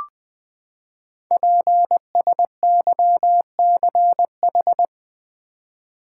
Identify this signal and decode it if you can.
Morse code, heard as a keyed tone, about 20 wpm, 700 Hz. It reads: PSYCH
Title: Build A Word - 5 Letters At A Time - 12-Letter Words - 20wpm